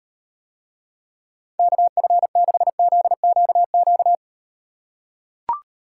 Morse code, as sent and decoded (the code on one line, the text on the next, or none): -.- ..-. -.... --.. --.- --.-
KF6ZQQ